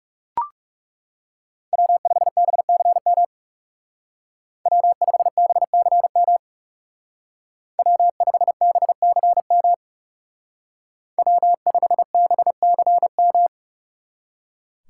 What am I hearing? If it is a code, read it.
Morse code, heard as a keyed tone, about 36 wpm, 700 Hz. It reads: W5BCM W5BCM W5BCM W5BCM
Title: Speed Racing - USA Call Signs 30wpm